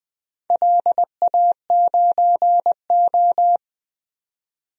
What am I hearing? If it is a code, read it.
Morse code, heard as a keyed tone, about 20 wpm, 700 Hz. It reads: LA9O
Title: DXCC-100 Call Signs - 4 Characters - 20wpm